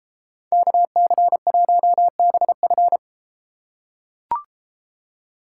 Morse code, heard as a keyed tone, about 33 wpm, 700 Hz. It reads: KC1BF